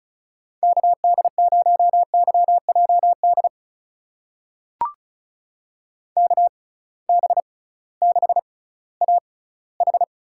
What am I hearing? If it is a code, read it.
Morse code, heard as a keyed tone, about 36 wpm, spaced as for 19 wpm, 700 Hz. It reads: KD0YJD KB6AH
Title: USA Call Signs - 6 Characters - 6x Character Spacing - 35wpm